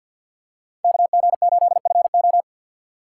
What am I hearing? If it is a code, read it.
Morse code, heard as a keyed tone, about 47 wpm, 690 Hz. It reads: KG8FO
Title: USA Call Signs - 5 Characters - 3x Character Spacing - 50wpm